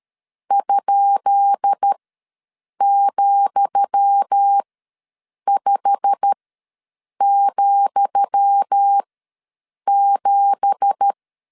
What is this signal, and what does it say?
Morse code, heard as a keyed tone, about 13 wpm, spaced as for 7 wpm, 800 Hz. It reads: ?,5,7
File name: LEZIONI / 10 SIGNS / 10D COMA ws 12wpm 3618